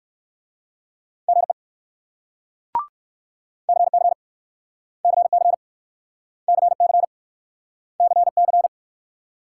Morse code, heard as a keyed tone, about 61 wpm, 690 Hz. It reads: GE CC CC CC CC